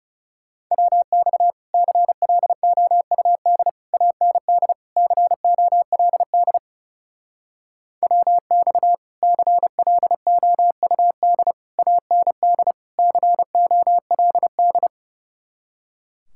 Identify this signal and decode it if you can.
Morse code, heard as a keyed tone, about 40 wpm, 700 Hz. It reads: WX CLOUD AND COLD WX CLOUD AND COLD